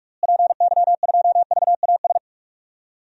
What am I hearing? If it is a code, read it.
Morse code, heard as a keyed tone, about 46 wpm, 700 Hz. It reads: PY2VAS
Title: DXCC-100 Call Signs - 6 Characters - Repeated 2x - 45wpm